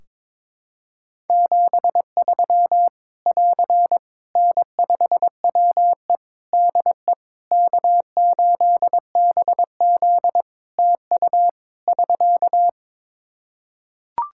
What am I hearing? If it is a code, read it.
Morse code, heard as a keyed tone, about 23 wpm, 700 Hz. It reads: 73 <AR> N5WE DE K8BZ TU <SK>
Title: K1USN SST QSO Practice 22wpm